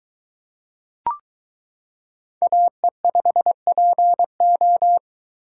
Morse code, heard as a keyed tone, about 22 wpm, 700 Hz. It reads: AE5PO